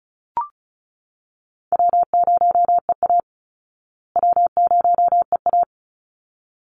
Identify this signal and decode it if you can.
Morse code, heard as a keyed tone, about 34 wpm, 700 Hz. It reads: W0EA W0EA
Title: Call Signs Repeated 3x - 35wpm